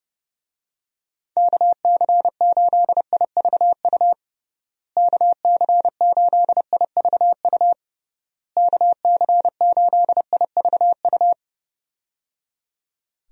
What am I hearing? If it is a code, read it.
Morse code, heard as a keyed tone, about 30 wpm, 700 Hz. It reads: KC8IVU KC8IVU KC8IVU
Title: USA Call Signs - 6 Characters - Repeated 3x - 30wpm